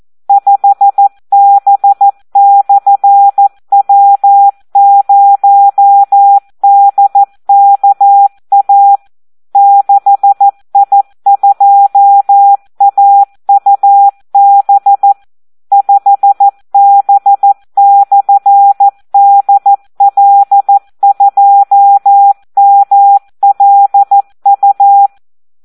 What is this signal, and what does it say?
Morse code, heard as a keyed tone, about 14 wpm, 800 Hz. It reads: 5B/W0DKA 6I2AUB 5B/DL2MLU
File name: Audio practica CW / SD 1 14